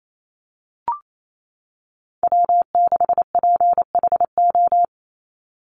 Morse code, heard as a keyed tone, about 28 wpm, 700 Hz. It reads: W6PHO
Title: Call Signs - 28wpm